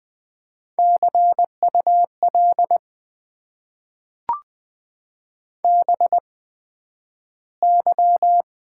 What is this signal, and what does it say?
Morse code, heard as a keyed tone, about 20 wpm, spaced as for 11 wpm, 700 Hz. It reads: CUL BY
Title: Top 90 QSO Elements - 8x Character Spacing - 20wpm